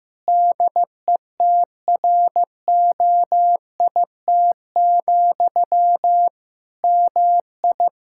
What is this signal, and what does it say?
Morse code, heard as a keyed tone, about 15 wpm, 700 Hz. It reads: DETROIT, MI
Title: Practice QSO - Exchange 1 - Beginning - 15wpm